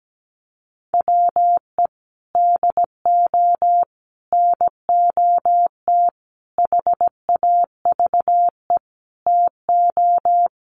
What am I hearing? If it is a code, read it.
Morse code, heard as a keyed tone, about 17 wpm, 700 Hz. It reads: WE DO NOT HAVE TO
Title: Sets of 5 Words 17wpm